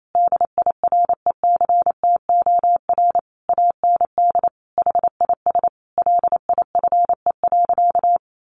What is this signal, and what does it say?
Morse code, heard as a keyed tone, about 28 wpm, 700 Hz. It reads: DIRECTOR AND HIS LIFE.
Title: Sentences from Top 1300 Words 28wpm